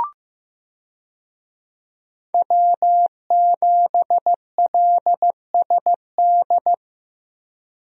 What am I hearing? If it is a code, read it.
Morse code, heard as a keyed tone, about 15 wpm, 700 Hz. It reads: W7LSD
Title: USA Call Signs - 5 Characters - Repeated 3x - 15wpm